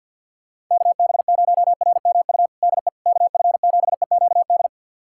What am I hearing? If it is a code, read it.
Morse code, heard as a keyed tone, about 51 wpm, 700 Hz. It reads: KB0RMU DE KF7EQD
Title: USA Callsign DE Callsign - 6 Characters - 50wpm